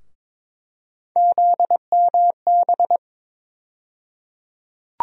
Morse code, heard as a keyed tone, about 22 wpm, 700 Hz. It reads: ZMB
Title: USA Call Sign Postfixes - 3 Letters 22wpm